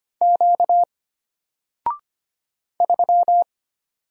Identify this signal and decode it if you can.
Morse code, heard as a keyed tone, about 25 wpm, 700 Hz. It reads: Q 3
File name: Single-Letter-Number-25wpm